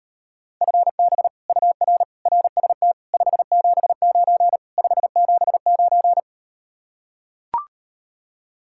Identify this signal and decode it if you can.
Morse code, heard as a keyed tone, about 38 wpm, 700 Hz. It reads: FB UR RST 579 579